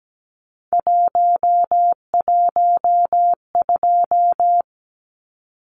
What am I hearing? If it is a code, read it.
Morse code, heard as a keyed tone, about 17 wpm, 700 Hz. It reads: 112
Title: Numbers - 3-Digit 17wpm